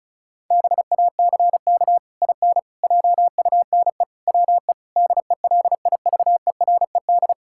Morse code, heard as a keyed tone, about 34 wpm, 700 Hz. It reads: BACK IN JUNE WE DELIVERED
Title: Pangram Challenge v2 - 35wpm